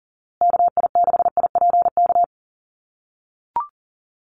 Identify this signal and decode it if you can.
Morse code, heard as a keyed tone, about 39 wpm, 700 Hz. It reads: KI6IPK
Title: USA Call Signs Repeated 2x 40wpm